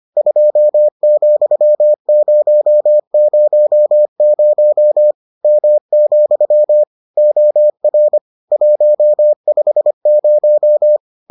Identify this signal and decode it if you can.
Morse code, heard as a keyed tone, about 25 wpm, 600 Hz. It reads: 2,000 M, OR 150